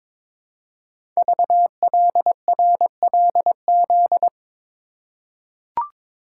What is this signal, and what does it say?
Morse code, heard as a keyed tone, about 22 wpm, 700 Hz. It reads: VLRLZ
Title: Code Groups - 5 Letters - 22wpm